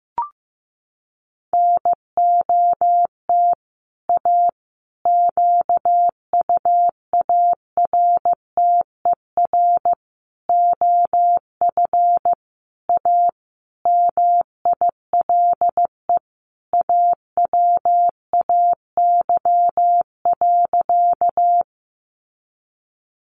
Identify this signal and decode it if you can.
Morse code, heard as a keyed tone, about 14 wpm, 700 Hz. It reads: NOT A QUARTER OF A MILE AWAY.
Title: Sentences from Top 1300 Words 15wpm